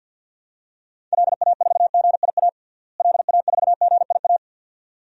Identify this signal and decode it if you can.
Morse code, heard as a keyed tone, about 51 wpm, 700 Hz. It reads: LA4GIA LA4GIA
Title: DXCC-100 Call Signs - 6 Characters - Repeated 2x - 50wpm